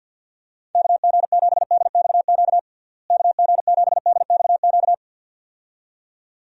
Morse code, heard as a keyed tone, about 50 wpm, 700 Hz. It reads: KG7DXQ KG7DXQ
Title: USA Call Signs - 6 Characters - Repeated 2x - 50wpm